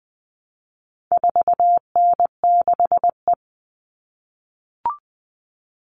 Morse code, heard as a keyed tone, about 20 wpm, 700 Hz. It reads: 4N6E
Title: All Cut Numbers - 4-Digit 20wpm